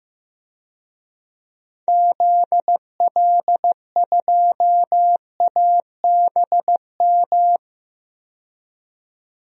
Morse code, heard as a keyed tone, about 16 wpm, 700 Hz. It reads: ZL2ABM
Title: DXCC-100 Call Signs - 6 Characters - 15wpm